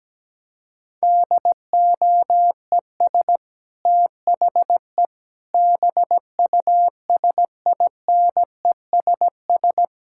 Morse code, heard as a keyed tone, about 17 wpm, 700 Hz. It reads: DOES THE BUSINESS
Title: Sentences from Top 1600 Words 17wpm